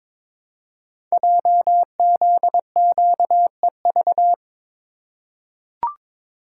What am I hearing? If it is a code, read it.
Morse code, heard as a keyed tone, about 22 wpm, 700 Hz. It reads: JZQEV
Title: Code Groups - 5 Letters - 22wpm